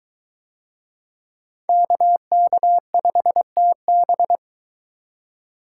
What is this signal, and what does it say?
Morse code, heard as a keyed tone, about 23 wpm, 700 Hz. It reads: KK5TB